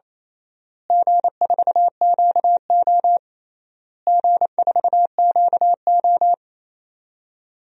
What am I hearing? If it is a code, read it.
Morse code, heard as a keyed tone, about 28 wpm, 700 Hz. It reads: G4QO G4QO
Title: DXCC-100 Call Signs - 4 Characters - Repeated 3x - 28wpm